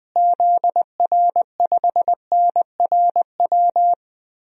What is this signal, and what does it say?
Morse code, heard as a keyed tone, about 20 wpm, 700 Hz. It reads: ZR5NRW
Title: DXCC-100 Call Signs - 6 Characters - Repeated 3x - 20wpm